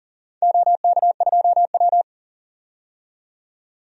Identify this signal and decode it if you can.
Morse code, heard as a keyed tone, about 40 wpm, 700 Hz. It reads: OK2W
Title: DXCC-100 Call Signs - 4 Characters - 40wpm